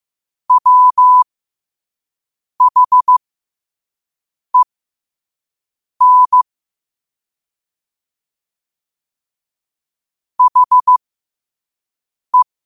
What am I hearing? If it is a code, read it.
Morse code, heard as a keyed tone, about 15 wpm, spaced as for 5 wpm, 1000 Hz. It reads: WHEN HE
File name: qotdfiles / 5wpm